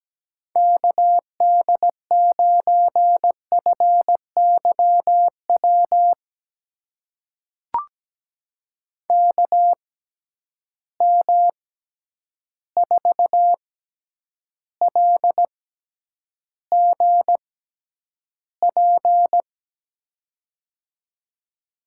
Morse code, heard as a keyed tone, about 17 wpm, spaced as for 9 wpm, 700 Hz. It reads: KD9FYW KM4LGP
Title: USA Call Signs - 6 Characters - 6x Character Spacing - 17wpm